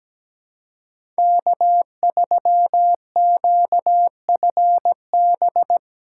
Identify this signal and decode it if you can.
Morse code, heard as a keyed tone, about 17 wpm, 700 Hz. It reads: K3QFB
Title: USA Call Signs - 2x Character Spacing - 17wpm